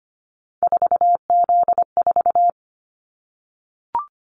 Morse code, heard as a keyed tone, about 25 wpm, 700 Hz. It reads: 4Z4
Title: Callsign Prefixes for Top 50 DXCC - 25wpm